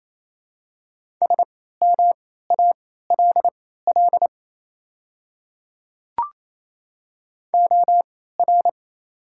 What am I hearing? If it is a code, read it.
Morse code, heard as a keyed tone, about 28 wpm, spaced as for 16 wpm, 700 Hz. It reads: SMALL OR
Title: 3x Character Spacing - Top 200 Words - 28wpm